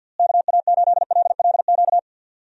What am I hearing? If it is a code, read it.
Morse code, heard as a keyed tone, about 49 wpm, 700 Hz. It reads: KA8LLQ
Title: USA Call Signs - 6 Characters - Repeated 3x - 50wpm